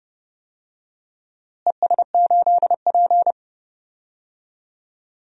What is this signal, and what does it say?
Morse code, heard as a keyed tone, about 30 wpm, 700 Hz. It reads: ES8P
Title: DXCC-100 Call Signs - 4 Characters - Repeated 2x - 30wpm